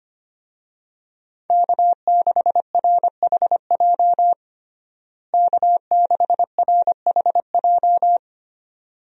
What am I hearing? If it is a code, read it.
Morse code, heard as a keyed tone, about 25 wpm, 700 Hz. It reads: K6RHJ K6RHJ
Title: USA Call Signs - 5 Characters - Repeated 2x - 25wpm